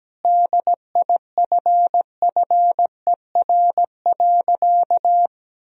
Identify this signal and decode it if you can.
Morse code, heard as a keyed tone, about 17 wpm, 700 Hz. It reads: DIFFER.
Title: Sentences from Top 1000 Words - Review Part 1 - 17wpm